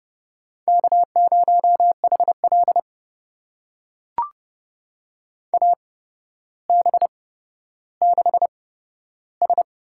Morse code, heard as a keyed tone, about 30 wpm, spaced as for 13 wpm, 700 Hz. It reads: K0HL AB6S
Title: USA Call Signs - 4 Characters - 8x Character Spacing - 30wpm